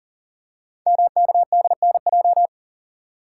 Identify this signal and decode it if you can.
Morse code, heard as a keyed tone, about 42 wpm, 690 Hz. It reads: MKDNJ
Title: Code Groups - 5 Letters - 40wpm